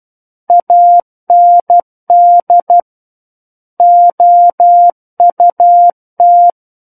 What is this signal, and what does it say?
Morse code, heard as a keyed tone, about 12 wpm, 700 Hz. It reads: AND OUT
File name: Book One / 03-Chapter 3 - 12 WPM